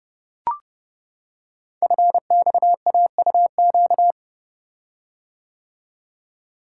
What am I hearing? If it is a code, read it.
Morse code, heard as a keyed tone, about 29 wpm, 700 Hz. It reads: FXAUQ
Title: Code Groups - 5 Letters - 30wpm